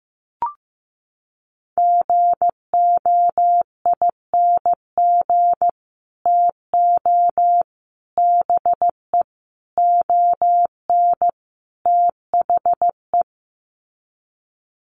Morse code, heard as a keyed tone, about 15 wpm, 700 Hz. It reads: GOING TO BE ON THE
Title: Sets of 5 Words Encore2 15wpm